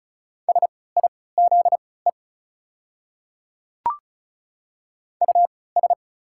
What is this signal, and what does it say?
Morse code, heard as a keyed tone, about 36 wpm, spaced as for 20 wpm, 690 Hz. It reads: SIZE US
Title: 3x Character Spacing - Top 500 Words - 35wpm